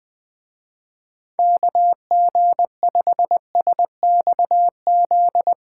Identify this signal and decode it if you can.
Morse code, heard as a keyed tone, about 21 wpm, 700 Hz. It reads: KG5SXZ